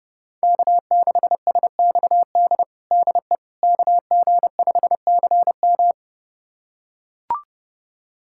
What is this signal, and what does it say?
Morse code, heard as a keyed tone, about 30 wpm, 700 Hz. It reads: K6SXD DE KG5CM
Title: USA Callsign DE Callsign - 5 Characters - 30wpm